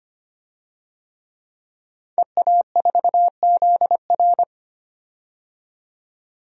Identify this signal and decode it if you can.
Morse code, heard as a keyed tone, about 25 wpm, 700 Hz. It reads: EA4ZR
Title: DXCC-100 Call Signs - 5 Characters - 25wpm